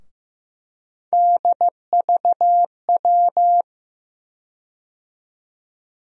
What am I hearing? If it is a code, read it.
Morse code, heard as a keyed tone, about 15 wpm, 700 Hz. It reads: DVW
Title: USA Call Sign Postfixes - 3 Letters 15wpm